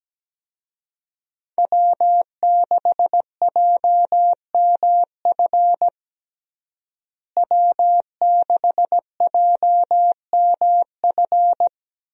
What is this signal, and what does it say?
Morse code, heard as a keyed tone, about 17 wpm, 700 Hz. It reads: W6JMF W6JMF
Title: USA Call Signs - 5 Characters - Repeated 3x - 17wpm